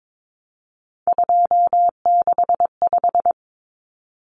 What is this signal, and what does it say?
Morse code, heard as a keyed tone, about 22 wpm, 700 Hz. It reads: 265
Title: Numbers - 3-Digit 22wpm